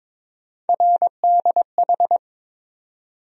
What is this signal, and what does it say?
Morse code, heard as a keyed tone, about 22 wpm, 700 Hz. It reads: RDH
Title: USA Call Sign Postfixes - 3 Letters 22wpm